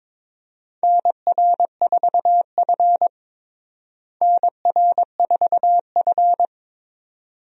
Morse code, heard as a keyed tone, about 21 wpm, 700 Hz. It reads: NR4F NR4F
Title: USA Call Signs - 4 Characters - Repeated 2x - 22wpm